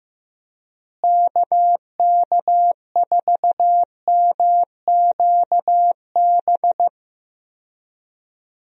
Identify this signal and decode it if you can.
Morse code, heard as a keyed tone, about 15 wpm, 700 Hz. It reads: KK4MQB
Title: USA Call Signs - 6 Characters - Repeated 2x - 15wpm